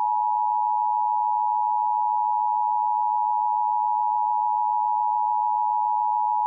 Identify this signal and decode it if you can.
Emergency Alert System (SAME) tones (the 853 and 960 Hz attention tone). It is heard from 0.9 to 6.2 s.